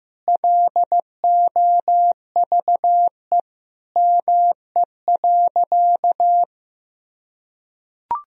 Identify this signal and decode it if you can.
Morse code, heard as a keyed tone, about 15 wpm, 700 Hz. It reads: LOVE ME.
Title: Sentences from Top 400 Words and Abbreviations 15wpm